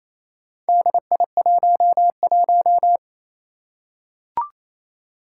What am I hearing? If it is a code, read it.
Morse code, heard as a keyed tone, about 28 wpm, 700 Hz. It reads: DI11
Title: Four Challenging Letter-Number Combinations 28wpm